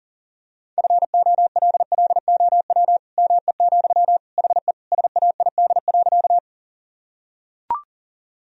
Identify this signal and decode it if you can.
Morse code, heard as a keyed tone, about 40 wpm, 700 Hz. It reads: FOLLOW ME, HE SAID.
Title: Sentences from Top 1000 Words - Review Part 1 - 40wpm